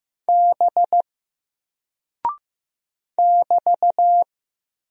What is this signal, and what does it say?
Morse code, heard as a keyed tone, about 15 wpm, 700 Hz. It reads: B <BT>
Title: Single Character (Letter:Number:Punctuation:Prosign) 15wpm